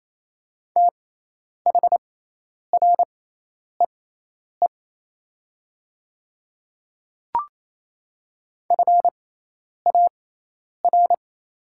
Morse code, heard as a keyed tone, about 27 wpm, spaced as for 10 wpm, 700 Hz. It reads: THREE FAR
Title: 6x Character Spacing - Top 300 Words - 28wpm